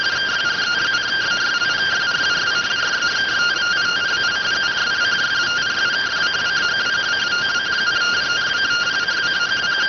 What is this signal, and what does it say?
Two-tone teleprinter signal, about 170 Hz shift.